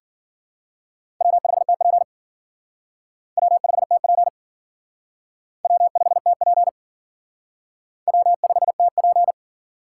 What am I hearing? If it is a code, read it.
Morse code, heard as a keyed tone, about 51 wpm, 700 Hz. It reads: W5TP W5TP W5TP W5TP